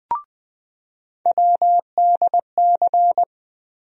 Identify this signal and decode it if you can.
Morse code, heard as a keyed tone, about 21 wpm, 700 Hz. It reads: WDC
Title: USA Call Sign Postfixes - 3 Letters 20wpm